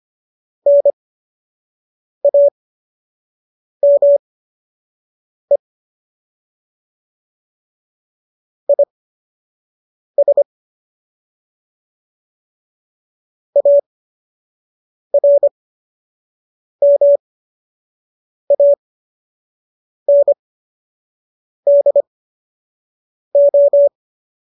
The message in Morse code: NAME IS ARMANDO

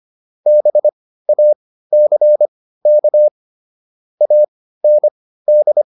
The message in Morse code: BACK AND

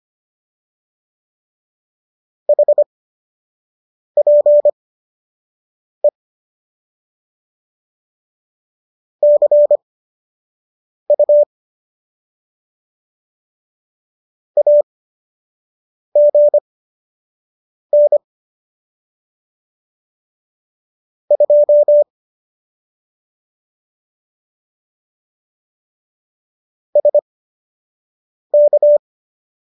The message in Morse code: HPE CU AGN 2 SK

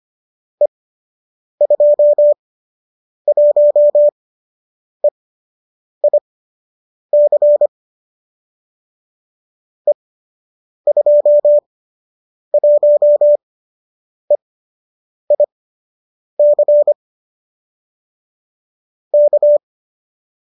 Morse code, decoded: E21EIC E21EIC K

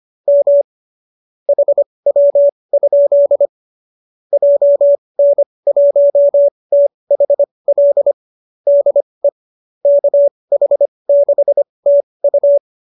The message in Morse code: M HW? JN1THL DE KH6TU

